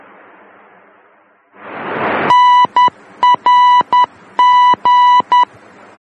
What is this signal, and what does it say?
Morse code, heard as a keyed tone, about 10 wpm, 990 Hz. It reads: NRG
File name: germany / mx / NRG-D